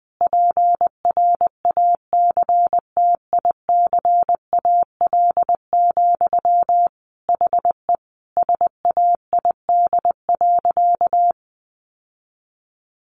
Morse code, heard as a keyed tone, about 20 wpm, 700 Hz. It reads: PRACTICAL, HE SAID.
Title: Sentences from Top 1400 Words 20wpm